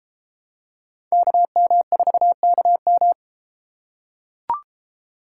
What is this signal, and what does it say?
Morse code, heard as a keyed tone, about 31 wpm, 700 Hz. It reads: KM4KM